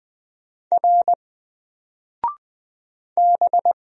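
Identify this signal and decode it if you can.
Morse code, heard as a keyed tone, about 20 wpm, 700 Hz. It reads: R B
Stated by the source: Single letter and number Morse drill at 20 wpm, from N4LCD.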